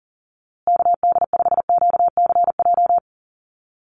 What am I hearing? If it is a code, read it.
Morse code, heard as a keyed tone, about 40 wpm, 700 Hz. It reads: KD5QCJ
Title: USA Call Signs Repeated 3x 40wpm